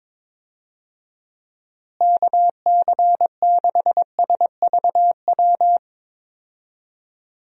Morse code, heard as a keyed tone, about 22 wpm, 700 Hz. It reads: KC6SVW
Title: USA Call Signs - 6 Characters - Repeated 2x - 22wpm